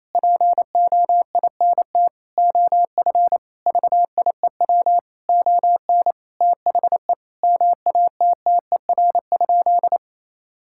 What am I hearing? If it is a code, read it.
Morse code, heard as a keyed tone, about 27 wpm, 700 Hz. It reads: POINT OF VIEW ON THE MATTER?
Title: Sentences from Top 1000 Words - Review Part 6 - 28wpm